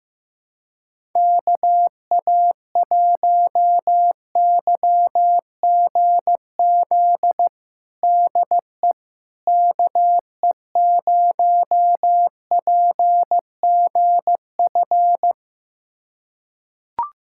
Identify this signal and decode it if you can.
Morse code, heard as a keyed tone, about 15 wpm, 700 Hz. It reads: KA1YGZ DE KE0PGF
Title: USA Callsign DE Callsign - 6 Characters - 15wpm